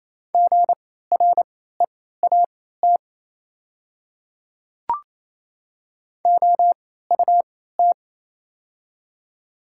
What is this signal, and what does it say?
Morse code, heard as a keyed tone, about 28 wpm, spaced as for 16 wpm, 700 Hz. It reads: GREAT OUT
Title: 3x Character Spacing - Top 200 Words - 28wpm